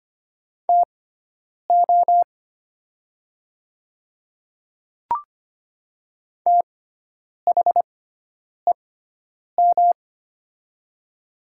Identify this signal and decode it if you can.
Morse code, heard as a keyed tone, about 25 wpm, spaced as for 9 wpm, 700 Hz. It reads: TO THEM
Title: 6x Character Spacing - Top 100 Words - 25wpm